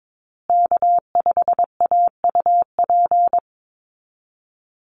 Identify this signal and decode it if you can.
Morse code, heard as a keyed tone, about 22 wpm, 700 Hz. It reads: K5AUP
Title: Call Signs Repeated 2x - 22wpm